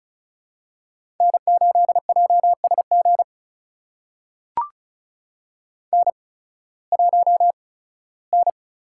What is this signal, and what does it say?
Morse code, heard as a keyed tone, about 36 wpm, spaced as for 19 wpm, 700 Hz. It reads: N8JSG N1N
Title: USA Call Signs - 5 Characters - 8x Character Spacing - 35wpm